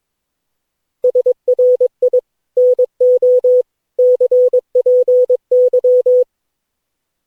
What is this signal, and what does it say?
Morse code, heard as a keyed tone, about 22 wpm, 500 Hz. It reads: SRI NO CPY
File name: cw / Session9 cw training